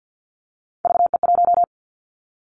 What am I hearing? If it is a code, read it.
Morse code, heard as a keyed tone, about 50 wpm, 700 Hz. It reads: VE1
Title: Callsign Prefixes for Top 25 DXCC - 50wpm